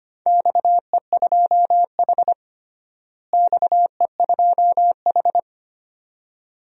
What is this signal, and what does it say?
Morse code, heard as a keyed tone, about 25 wpm, 700 Hz. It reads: XE2H XE2H
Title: DXCC-100 Call Signs - 4 Characters - Repeated 2x - 25wpm